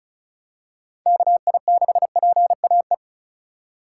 Morse code, heard as a keyed tone, about 36 wpm, 700 Hz. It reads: KI6PAE
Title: USA Call Signs - 2x Character Spacing - 35wpm